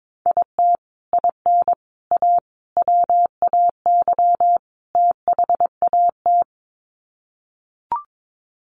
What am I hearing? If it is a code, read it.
Morse code, heard as a keyed tone, about 22 wpm, 700 Hz. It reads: IT IN A WAY THAT
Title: Sets of 5 Words Encore2 22wpm